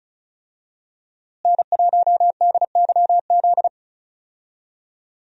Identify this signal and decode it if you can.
Morse code, heard as a keyed tone, about 34 wpm, 700 Hz. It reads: N1DYZ
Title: USA Call Signs - 5 Characters - Repeated 3x - 35wpm